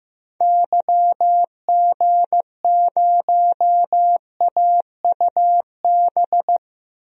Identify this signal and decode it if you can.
Morse code, heard as a keyed tone, about 15 wpm, 700 Hz. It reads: YG0AUB
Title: DXCC-100 Call Signs - 6 Characters - 15wpm